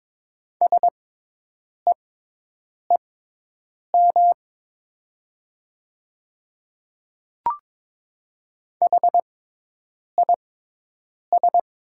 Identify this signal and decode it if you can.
Morse code, heard as a keyed tone, about 22 wpm, spaced as for 8 wpm, 700 Hz. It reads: SEEM HIS
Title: 6x Character Spacing - Top 300 Words - 22wpm